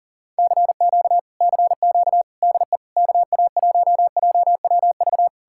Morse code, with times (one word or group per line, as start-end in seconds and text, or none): 0.4-1.2 CQ
1.4-2.2 CQ
2.4-2.8 DE
3.0-5.3 KA1JWV